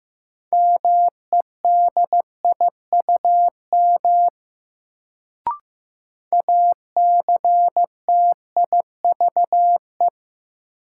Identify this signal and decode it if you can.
Morse code, heard as a keyed tone, about 15 wpm, 700 Hz. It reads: MEDIUM ACTIVE
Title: Sentences from Top 900 Words - New Word Drill - 15wpm